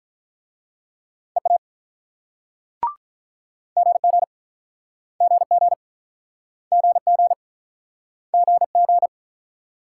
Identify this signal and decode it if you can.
Morse code, heard as a keyed tone, about 47 wpm, 690 Hz. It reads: EA GG GG GG GG